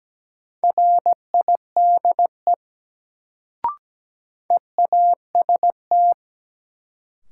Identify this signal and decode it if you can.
Morse code, heard as a keyed tone, about 17 wpm, 700 Hz. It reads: RIDE EAST
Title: Sentences from Top 1200 Words - New Word Drill Alt - 17wpm